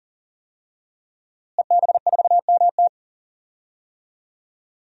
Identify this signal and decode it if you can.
Morse code, heard as a keyed tone, about 40 wpm, 700 Hz. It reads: EB4MT
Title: DXCC-100 Call Signs - 5 Characters - Repeated 3x - 40wpm